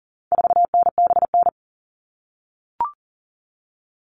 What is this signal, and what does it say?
Morse code, heard as a keyed tone, about 37 wpm, 700 Hz. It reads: 4NBN
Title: All Cut Numbers - 4-Digit 40wpm